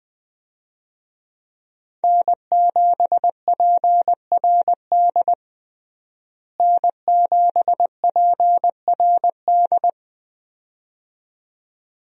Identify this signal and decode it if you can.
Morse code, heard as a keyed tone, about 20 wpm, 700 Hz. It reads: N7PRD N7PRD
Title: USA Call Signs - 5 Characters - Repeated 2x - 20wpm